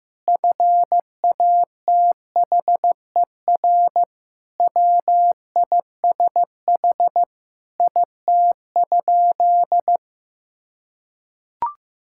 Morse code, heard as a keyed tone, about 16 wpm, 700 Hz. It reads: FATHER WISH IT?
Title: Sentences from Top 1600 Words 15wpm